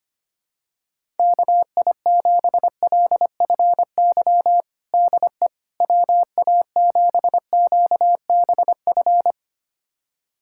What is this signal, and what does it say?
Morse code, heard as a keyed tone, about 25 wpm, 700 Hz. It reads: KI7LFY DE WA7QBF
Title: USA Callsign DE Callsign 25wpm